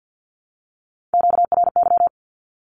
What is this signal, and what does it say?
Morse code, heard as a keyed tone, about 50 wpm, 700 Hz. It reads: QRY
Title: Q-Codes Repeated 3x 50wpm